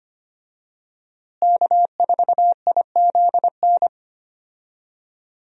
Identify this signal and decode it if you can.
Morse code, heard as a keyed tone, about 25 wpm, 700 Hz. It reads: K4IZN
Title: USA Call Signs - 5 Characters - Repeated 2x - 25wpm